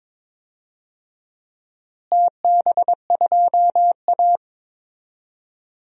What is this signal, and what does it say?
Morse code, heard as a keyed tone, about 22 wpm, 700 Hz. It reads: TB2A
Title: DXCC-100 Call Signs - 4 Characters - Repeated 3x - 22wpm